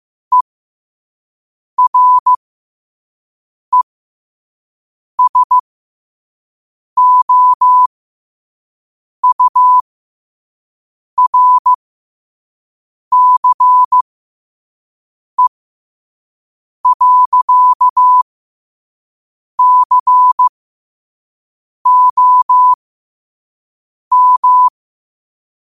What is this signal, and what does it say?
Morse code, heard as a keyed tone, about 15 wpm, spaced as for 5 wpm, 1000 Hz. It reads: ERESOURCE.COM